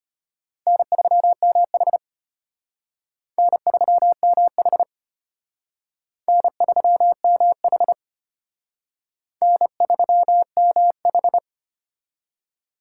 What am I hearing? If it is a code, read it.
Morse code, heard as a keyed tone, about 32 wpm, 700 Hz. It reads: N3MH N3MH N3MH N3MH